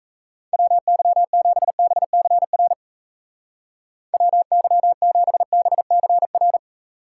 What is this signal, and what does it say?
Morse code, heard as a keyed tone, about 34 wpm, 700 Hz. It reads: WY7BCR WY7BCR
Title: Speed Racing - USA 2x3 Call Signs 28wpm